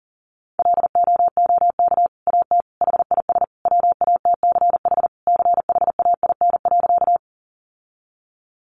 Morse code, heard as a keyed tone, about 40 wpm, 700 Hz. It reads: LOOK AT HIS WATCH CHAIN.
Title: Sentences from Top 1500 Words 40wpm